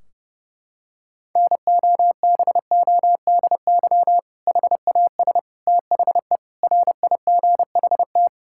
Morse code, heard as a keyed tone, about 30 wpm, 700 Hz. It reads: NOBODY HAS THE RIGHT
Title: Sentences from Top 1600 Words 30wpm